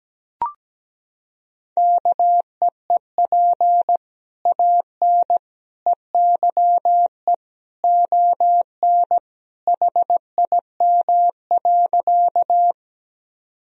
Morse code, heard as a keyed tone, about 17 wpm, 700 Hz. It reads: KEEP AN EYE ON HIM.
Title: Sentences from Top 1000 Words - Review Part 1 - 17wpm